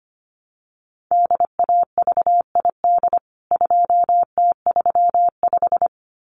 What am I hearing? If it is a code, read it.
Morse code, heard as a keyed tone, about 25 wpm, 700 Hz. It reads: DAVID 2T35
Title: CWT Exchange 25wpm